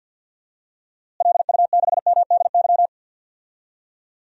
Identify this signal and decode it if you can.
Morse code, heard as a keyed tone, about 46 wpm, 700 Hz. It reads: LU6MDY
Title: DXCC-100 Call Signs - 6 Characters - Repeated 3x - 50wpm